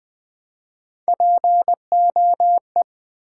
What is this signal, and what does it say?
Morse code, heard as a keyed tone, about 20 wpm, 700 Hz. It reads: POE
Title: Build A Word - 1 Letter At A Time - 4-Letter Words - 20wpm